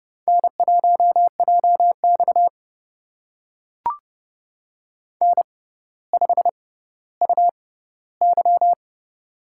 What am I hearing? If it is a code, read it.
Morse code, heard as a keyed tone, about 29 wpm, spaced as for 15 wpm, 700 Hz. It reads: N1JX N5UY
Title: USA Call Signs - 4 Characters - 6x Character Spacing - 30wpm